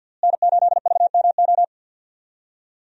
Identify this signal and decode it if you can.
Morse code, heard as a keyed tone, about 53 wpm, 700 Hz. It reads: N8VMO
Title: USA Call Signs - 5 Characters - Repeated 3x - 50wpm